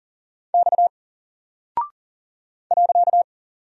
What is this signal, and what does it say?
Morse code, heard as a keyed tone, about 40 wpm, 700 Hz. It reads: X .
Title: Single Character (Letter:Number:Punctuation:Prosign) Alt 40wpm